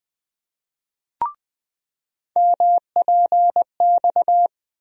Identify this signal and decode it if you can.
Morse code, heard as a keyed tone, about 20 wpm, 700 Hz. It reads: MPX
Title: USA Call Sign Postfixes - 3 Letters 20wpm